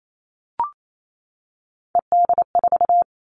Morse code, heard as a keyed tone, about 28 wpm, 700 Hz. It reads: ED4
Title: All Cut Numbers - 3-Digit 28wpm